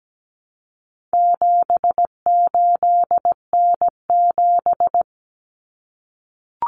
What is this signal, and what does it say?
Morse code, heard as a keyed tone, about 17 wpm, 700 Hz. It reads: 78N7
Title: Common Cut Numbers - 4-Digit 17wpm